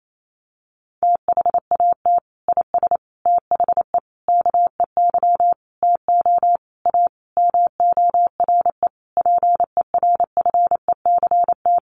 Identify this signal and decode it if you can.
Morse code, heard as a keyed tone, about 28 wpm, 700 Hz. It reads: THAT IS THE KEY TO A MORE PERFECT
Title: Sentences from Top 1400 Words 28wpm